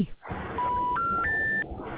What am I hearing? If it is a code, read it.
Special information tone, with speech.
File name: recordings / RE75d5cc726d28d6cc7e8ed0fdf7c90f33